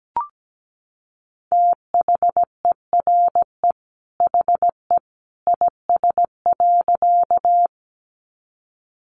Morse code, heard as a keyed tone, about 17 wpm, 700 Hz. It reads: THERE HE IS.
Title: A Sound Of Thunder 17wpm